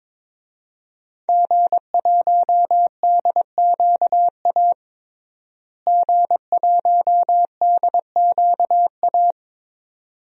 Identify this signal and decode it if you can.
Morse code, heard as a keyed tone, about 22 wpm, 700 Hz. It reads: G1DQA G1DQA
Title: DXCC-100 Call Signs - 5 Characters - Repeated 2x - 22wpm